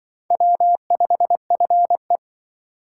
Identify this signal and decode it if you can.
Morse code, heard as a keyed tone, about 23 wpm, 700 Hz. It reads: W5FE
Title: Speed Racing - USA Call Signs 20wpm